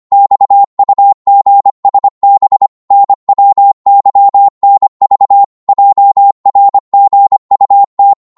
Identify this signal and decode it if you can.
Morse code, heard as a keyed tone, about 25 wpm, 800 Hz. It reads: XUGSB NWYNV JRGUT
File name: Morse / ltr-25wpm